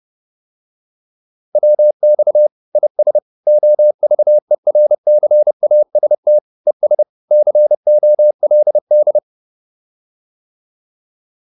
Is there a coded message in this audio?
Yes: Morse code from 1.6 to 9.2 s, heard as a keyed tone, about 30 wpm, 600 Hz. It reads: WX IS OVERCAST ES COLD